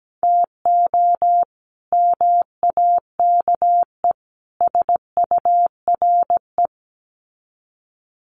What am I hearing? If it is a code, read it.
Morse code, heard as a keyed tone, about 17 wpm, 700 Hz. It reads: TO MAKE SURE
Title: Sets of 3 Words 17wpm